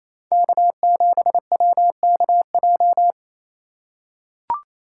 Morse code, heard as a keyed tone, about 28 wpm, 700 Hz. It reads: K7WKJ
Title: USA Call Signs - 5 Characters - 2x Character Spacing - 28wpm